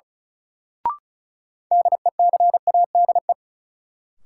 Morse code, heard as a keyed tone, about 36 wpm, 700 Hz. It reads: DECADE
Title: Sets of 5 Words - New Word Drill - 35wpm